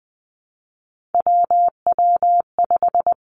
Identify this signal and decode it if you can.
Morse code, heard as a keyed tone, about 20 wpm, 700 Hz. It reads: WW5
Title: Three Challenging Letter-Number Combinations 20wpm